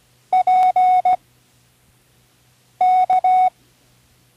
Morse code, heard as a keyed tone, about 16 wpm, 740 Hz. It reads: P K
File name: CWResources / Learning Morse Code (from Tune in the World) / 13 random